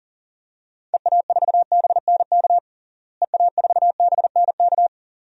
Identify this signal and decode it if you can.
Morse code, heard as a keyed tone, about 37 wpm, 700 Hz. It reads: EA4BNK EA4BNK
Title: DXCC-100 Call Signs - 6 Characters - Repeated 2x - 40wpm